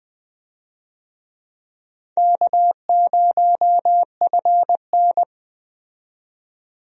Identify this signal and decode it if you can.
Morse code, heard as a keyed tone, about 20 wpm, 700 Hz. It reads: K0FN
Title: USA Call Signs - 4 Characters - 20wpm